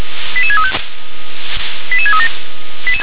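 Five-tone selective call (ZVEI), heard as quick runs of five tones, tones of about 70 ms. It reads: 88635 88637
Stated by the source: ZVEI five-tone selective call received with an SDR.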